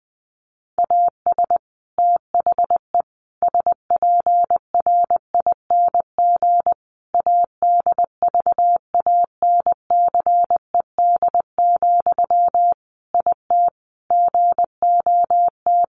Morse code, heard as a keyed tone, about 20 wpm, 700 Hz. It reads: AS THE SPRING ADVANCED, IT GOT
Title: Sentences from Top 1200 Words 20wpm